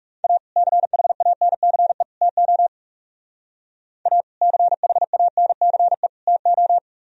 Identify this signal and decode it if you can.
Morse code, heard as a keyed tone, about 36 wpm, 700 Hz. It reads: A CHANCE TO A CHANCE TO
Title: Speed-Racing - Sets of 3 Words 30wpm